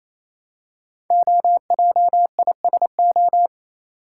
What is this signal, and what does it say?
Morse code, heard as a keyed tone, about 29 wpm, 700 Hz. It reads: OJISO
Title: Code Groups - 5 Letters - 28wpm